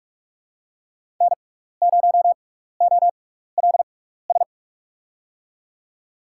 Morse code, heard as a keyed tone, about 44 wpm, spaced as for 15 wpm, 700 Hz. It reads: N0OLS
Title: USA Call Signs - 5 Characters - 6x Character Spacing - 45wpm